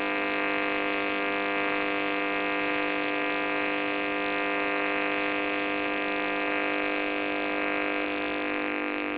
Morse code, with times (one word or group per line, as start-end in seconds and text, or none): none